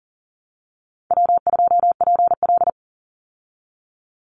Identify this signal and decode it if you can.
Morse code, heard as a keyed tone, about 40 wpm, 700 Hz. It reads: W2PL
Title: Easy Call Signs Repeated 2x - 40wpm